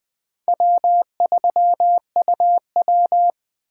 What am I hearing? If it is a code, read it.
Morse code, heard as a keyed tone, about 20 wpm, 700 Hz. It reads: W3UW
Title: Speed Racing - USA 1x2 Call Signs 17wpm